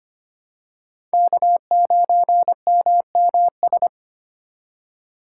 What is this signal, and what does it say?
Morse code, heard as a keyed tone, about 25 wpm, 700 Hz. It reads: K9MMS
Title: K1USN SST QSO Practice - 25wpm